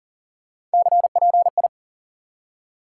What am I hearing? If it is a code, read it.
Morse code, heard as a keyed tone, about 40 wpm, 700 Hz. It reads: CPI
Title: Top 90 QSO Elements 40wpm